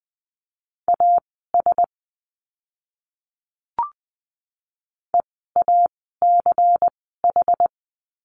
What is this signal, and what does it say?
Morse code, heard as a keyed tone, about 20 wpm, spaced as for 15 wpm, 700 Hz. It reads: AS EACH
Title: 192 Lesson 31f - Letter v - Words 20wpm